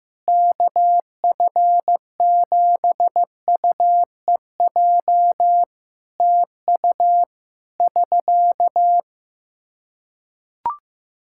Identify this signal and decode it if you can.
Morse code, heard as a keyed tone, about 15 wpm, 700 Hz. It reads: KF7UEJ TU <SK>
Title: K1USN SST QSO Practice 15wpm